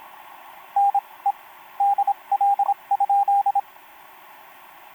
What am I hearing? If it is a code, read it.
Morse code, heard as a keyed tone, about 26 wpm, 800 Hz, with radio noise. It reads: N E DL?
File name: pj4-w9njy f5upo etc 12m cw 28mar2011 1924